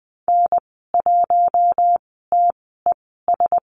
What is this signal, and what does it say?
Morse code, heard as a keyed tone, about 20 wpm, spaced as for 14 wpm, 700 Hz. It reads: N1TES
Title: 042 Lesson 8g - Number 1 - Call Signs 20wpm